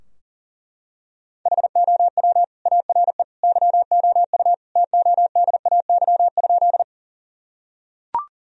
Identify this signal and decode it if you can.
Morse code, heard as a keyed tone, about 40 wpm, 700 Hz. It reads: HOW ARE YOU TODAY?
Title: Sentences from Top 1000 Words - Review Part 4 - 40wpm